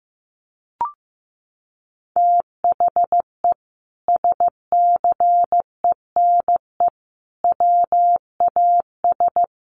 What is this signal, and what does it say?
Morse code, heard as a keyed tone, about 15 wpm, 700 Hz. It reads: THE SCENE WAS
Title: Sentences from Top 1100 Words 15wpm